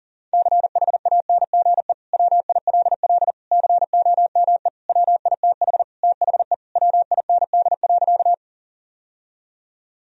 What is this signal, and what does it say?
Morse code, heard as a keyed tone, about 37 wpm, 700 Hz. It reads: CHANGE WILL COME WITH THE WIND.
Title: Sentences from Top 1000 Words - Review Part 2 - 40wpm